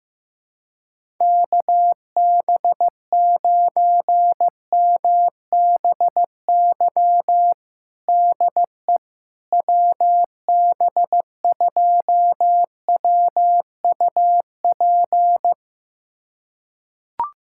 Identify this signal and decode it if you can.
Morse code, heard as a keyed tone, about 16 wpm, 700 Hz. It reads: KB9MBY DE WB2WUP
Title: USA Callsign DE Callsign - 6 Characters - 15wpm